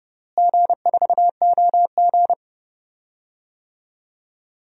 Morse code, heard as a keyed tone, about 30 wpm, 700 Hz. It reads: G4OG
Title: DXCC-100 Call Signs - 4 Characters - Repeated 3x - 30wpm